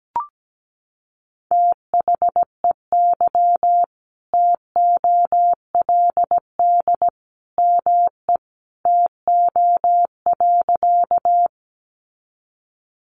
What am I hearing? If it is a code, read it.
Morse code, heard as a keyed tone, about 17 wpm, 700 Hz. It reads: THEY TOLD ME TO.
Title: Sentences from Top 400 Words 17wpm